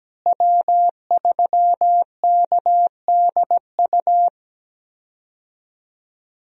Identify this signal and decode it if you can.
Morse code, heard as a keyed tone, about 17 wpm, 700 Hz. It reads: W3KDU
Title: USA Call Signs - 5 Characters - Repeated 3x - 17wpm